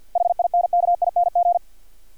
Morse code, heard as a keyed tone, about 50 wpm, 700 Hz. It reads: HINKING
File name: archived / CRSnet